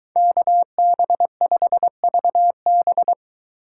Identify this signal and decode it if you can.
Morse code, heard as a keyed tone, about 23 wpm, 700 Hz. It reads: KB5VB